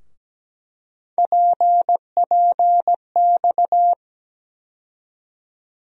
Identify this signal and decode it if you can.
Morse code, heard as a keyed tone, about 17 wpm, 700 Hz. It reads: PPX
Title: USA Call Sign Postfixes - 3 Letters 17wpm